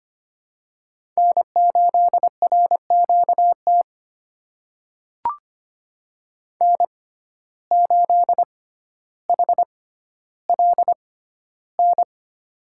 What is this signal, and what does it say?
Morse code, heard as a keyed tone, about 25 wpm, spaced as for 13 wpm, 700 Hz. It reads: N8RQT N8HLN
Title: USA Call Signs - 5 Characters - 6x Character Spacing - 25wpm